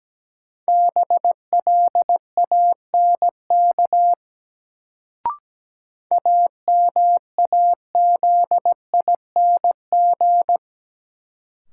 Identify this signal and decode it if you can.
Morse code, heard as a keyed tone, about 17 wpm, 700 Hz. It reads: BLANK AMAZING
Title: Sentences from Top 1500 Words - New Word Drill Alt - 17wpm